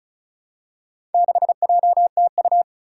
Morse code, heard as a keyed tone, about 36 wpm, 700 Hz. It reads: 6JTU
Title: Four Challenging Letter-Number Combinations 35wpm